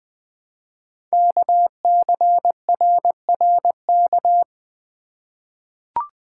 Morse code, heard as a keyed tone, about 20 wpm, 700 Hz. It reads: KCRRK
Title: Code Groups - 5 Letter:Numbers - 20wpm